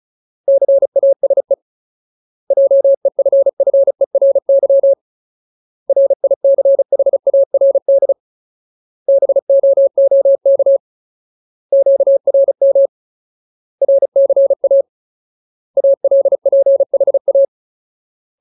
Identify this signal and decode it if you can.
Morse code, heard as a keyed tone, about 34 wpm, 550 Hz. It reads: CASE JEFFERY RICHARD BOOK QRM RCA ALPHA
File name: Ultimate Word List / 35WPM Ultimate